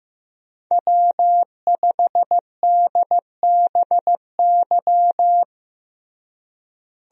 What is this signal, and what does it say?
Morse code, heard as a keyed tone, about 15 wpm, 700 Hz. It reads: W5DBY
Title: USA Call Signs - 5 Characters - Repeated 2x - 15wpm